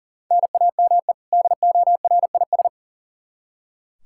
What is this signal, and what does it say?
Morse code, heard as a keyed tone, about 40 wpm, 700 Hz. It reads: NAME DORIS
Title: Practice QSO - Exchange 1 - Beginning - 40wpm